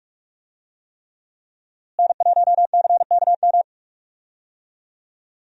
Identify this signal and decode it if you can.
Morse code, heard as a keyed tone, about 41 wpm, 700 Hz. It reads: N1CKM
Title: USA Call Signs - 5 Characters - Repeated 3x - 45wpm